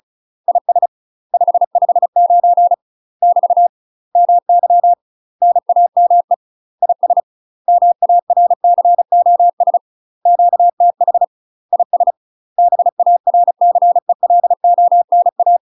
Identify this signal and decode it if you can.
Morse code, heard as a keyed tone, about 34 wpm, 700 Hz. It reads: IS 559 <BT> MY NAME IS MARCOS QTH IS BARCELONA
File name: UK MASTER QSOs TRAINING / 35WPM / UK MASTER QSOs 0040